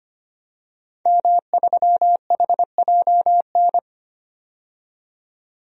Morse code, heard as a keyed tone, about 25 wpm, 700 Hz. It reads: M3HJN
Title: DXCC-100 Call Signs - 5 Characters - 25wpm